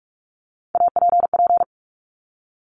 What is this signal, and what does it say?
Morse code, heard as a keyed tone, about 46 wpm, 700 Hz. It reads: APP
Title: Top Three Letter Combinations in Words 45wpm